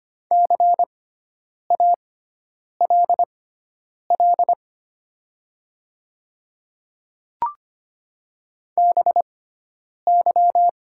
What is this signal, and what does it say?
Morse code, heard as a keyed tone, about 25 wpm, spaced as for 9 wpm, 700 Hz. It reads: CALL BY
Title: 6x Character Spacing - Top 100 Words - 25wpm